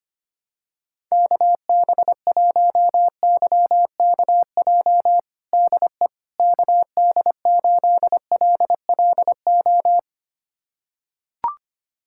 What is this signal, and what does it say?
Morse code, heard as a keyed tone, about 25 wpm, 700 Hz. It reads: KB1YKJ DE KD8LLO
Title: USA Callsign DE Callsign - 6 Characters - 25wpm